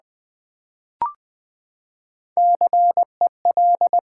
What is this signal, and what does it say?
Morse code, heard as a keyed tone, about 20 wpm, 700 Hz. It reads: CEL
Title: Build A Word - 3 Letters At A Time - 9-Letter Words - 20wpm